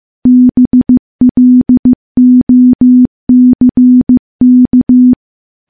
Morse code, heard as a keyed tone, about 16 wpm, 250 Hz. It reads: BLOCK
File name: morse / morse-training-15WPM